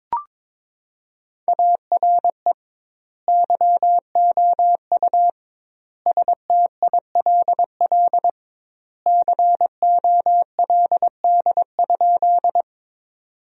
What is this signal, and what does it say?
Morse code, heard as a keyed tone, about 22 wpm, 700 Hz. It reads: ARE YOU STILL COLD?
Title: 2x Word-Spacing - Sentences from Top 400 Words - 22wpm